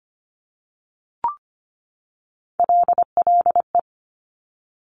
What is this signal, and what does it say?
Morse code, heard as a keyed tone, about 25 wpm, 700 Hz. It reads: LLE
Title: Top Three Letter Combinations in Words 25wpm